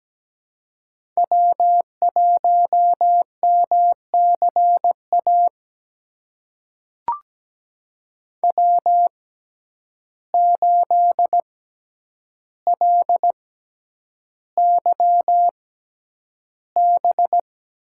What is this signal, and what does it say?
Morse code, heard as a keyed tone, about 17 wpm, spaced as for 9 wpm, 700 Hz. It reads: W1MCA W8LYB
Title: USA Call Signs - 6x Character Spacing - 17wpm